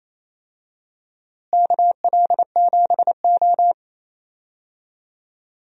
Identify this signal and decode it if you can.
Morse code, heard as a keyed tone, about 27 wpm, 700 Hz. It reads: KL7O
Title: DXCC-100 Call Signs - 4 Characters - Repeated 3x - 28wpm